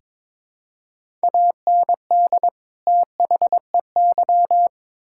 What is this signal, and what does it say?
Morse code, heard as a keyed tone, about 21 wpm, 700 Hz. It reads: AND THEY
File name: Build A Phrase - 2-Word Phrases - 22wpm